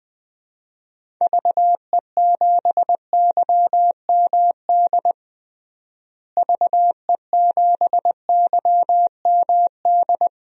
Morse code, heard as a keyed tone, about 20 wpm, 700 Hz. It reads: VE7YMD VE7YMD
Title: DXCC-100 Call Signs - 6 Characters - Repeated 2x - 20wpm